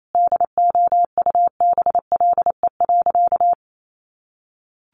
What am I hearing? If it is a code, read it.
Morse code, heard as a keyed tone, about 28 wpm, 700 Hz. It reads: DOUBLE.
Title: Sentences from Top 1400 Words 28wpm